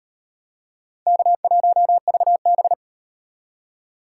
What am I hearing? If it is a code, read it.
Morse code, heard as a keyed tone, about 36 wpm, 700 Hz. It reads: K1VB